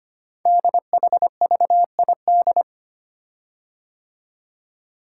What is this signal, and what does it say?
Morse code, heard as a keyed tone, about 25 wpm, 700 Hz. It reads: DHVID
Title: Code Groups - 5 Letters - 25wpm